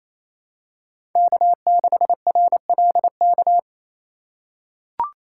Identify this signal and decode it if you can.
Morse code, heard as a keyed tone, about 28 wpm, 700 Hz. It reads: K6RLK
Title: USA Call Signs - 2x Character Spacing - 28wpm